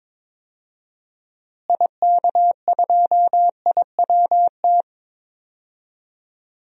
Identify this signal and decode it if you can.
Morse code, heard as a keyed tone, about 22 wpm, 700 Hz. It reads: IK2IWT
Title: DXCC-100 Call Signs - 6 Characters - Repeated 3x - 22wpm